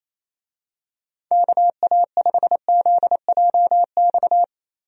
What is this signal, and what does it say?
Morse code, heard as a keyed tone, about 28 wpm, 700 Hz. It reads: KA5ZJX
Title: USA Call Signs - 6 Characters - Repeated 3x - 28wpm